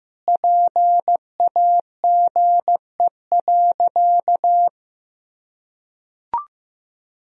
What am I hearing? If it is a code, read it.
Morse code, heard as a keyed tone, about 15 wpm, 700 Hz. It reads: PAGE.
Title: Sentences from Top 1000 Words - Review Part 1 - 15wpm